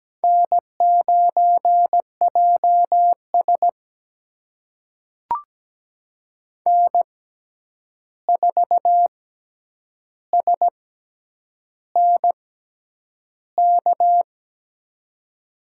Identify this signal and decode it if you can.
Morse code, heard as a keyed tone, about 17 wpm, spaced as for 8 wpm, 700 Hz. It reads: N9JS N4SNK
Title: USA Call Signs - 6x Character Spacing - 17wpm